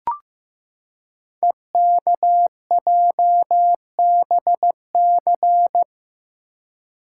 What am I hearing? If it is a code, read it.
Morse code, heard as a keyed tone, about 15 wpm, 700 Hz. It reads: EKJBC
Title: Code Groups - 5 Letters - 15wpm